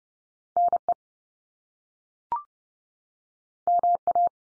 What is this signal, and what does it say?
Morse code, heard as a keyed tone, about 30 wpm, 700 Hz. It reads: NE MA
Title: US States 30WPM